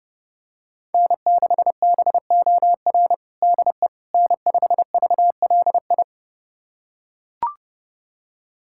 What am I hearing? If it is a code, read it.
Morse code, heard as a keyed tone, about 31 wpm, 700 Hz. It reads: N6BOR DE N5VLI
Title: USA Callsign DE Callsign - 5 Characters - 30wpm